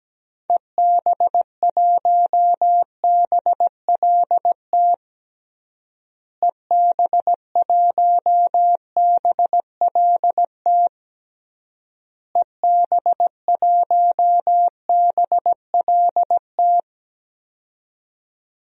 Morse code, heard as a keyed tone, about 17 wpm, 700 Hz. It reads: EB1BLT EB1BLT EB1BLT
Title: DXCC-100 Call Signs - 6 Characters - Repeated 3x - 17wpm